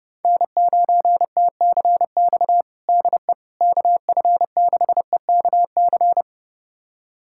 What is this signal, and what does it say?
Morse code, heard as a keyed tone, about 30 wpm, 700 Hz. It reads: N9TCX DE KF6EKC
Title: USA Callsign DE Callsign 30wpm